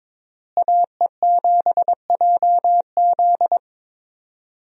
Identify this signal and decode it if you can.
Morse code, heard as a keyed tone, about 22 wpm, 700 Hz. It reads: AE7JZ
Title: USA Call Signs - 5 Characters - Repeated 3x - 22wpm